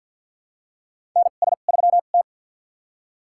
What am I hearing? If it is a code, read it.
Morse code, heard as a keyed tone, about 48 wpm, spaced as for 35 wpm, 700 Hz. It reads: NS3T
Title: USA Call Signs - 2x Character Spacing - 50wpm